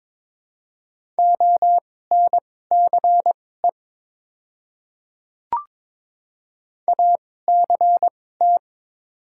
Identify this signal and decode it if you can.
Morse code, heard as a keyed tone, about 22 wpm, spaced as for 16 wpm, 700 Hz. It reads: ONCE ACT
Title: 2x Character Spacing - Top 500 Words - 22wpm